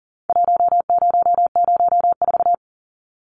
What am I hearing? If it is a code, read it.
Morse code, heard as a keyed tone, about 40 wpm, 700 Hz. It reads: 1004
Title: Numbers - 4-Digit 40wpm